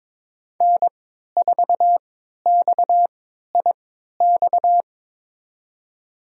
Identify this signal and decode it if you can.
Morse code, heard as a keyed tone, about 23 wpm, spaced as for 13 wpm, 700 Hz. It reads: N4XIX
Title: USA Call Signs - 3x Character Spacing - 22wpm